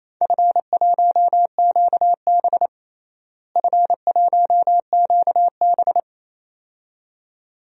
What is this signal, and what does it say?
Morse code, heard as a keyed tone, about 27 wpm, 700 Hz. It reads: F1QB F1QB
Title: DXCC-100 Call Signs - 4 Characters - Repeated 2x - 28wpm